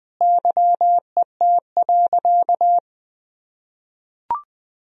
Morse code, heard as a keyed tone, about 19 wpm, 700 Hz. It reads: YET.
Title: Sentences from Top 1000 Words - Review Part 4 - 20wpm